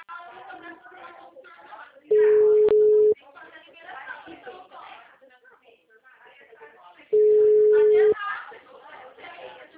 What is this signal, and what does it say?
Ringback tone (Europe, 425 Hz), with speech.
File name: out- -2009- - - .38297